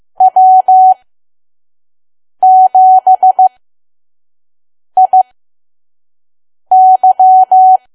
Morse code, heard as a keyed tone, about 15 wpm, spaced as for 5 wpm, 750 Hz. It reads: W7IY
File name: Archive / 5 WPM files / PM